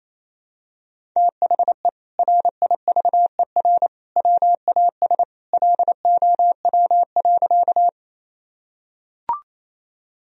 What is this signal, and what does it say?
Morse code, heard as a keyed tone, about 28 wpm, 700 Hz. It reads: THE RIVER WAS LOW.
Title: Sentences from Top 1000 Words - Review Part 1 - 28wpm